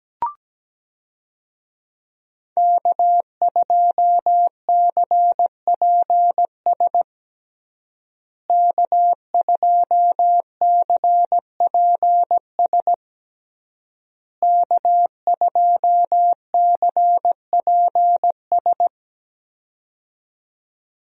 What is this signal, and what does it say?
Morse code, heard as a keyed tone, about 17 wpm, 700 Hz. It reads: K2CPS K2CPS K2CPS
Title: USA Call Signs - 5 Characters - Repeated 3x - 17wpm